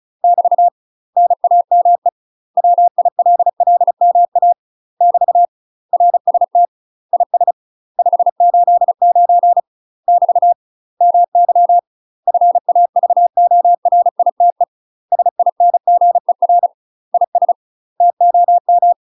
Morse code, heard as a keyed tone, about 36 wpm, 700 Hz. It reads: <BT> NAME WILLMA <BT> RST IS 589 <BT> MY FAVORITE SINGER IS TOM
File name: UK MASTER QSOs TRAINING / 35WPM / UK MASTER QSOs 0036